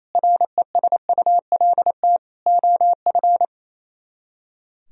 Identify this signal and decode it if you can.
Morse code, heard as a keyed tone, about 35 wpm, 700 Hz. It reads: RESULT OF